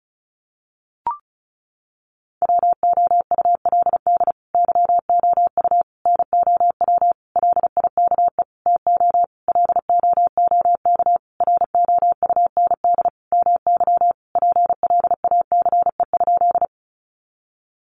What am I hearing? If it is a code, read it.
Morse code, heard as a keyed tone, about 34 wpm, 700 Hz. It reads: WOULD YOU NOW LIKE TO LOOK ROUND MY PLACE?
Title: Sentences from Top 200 Words 35wpm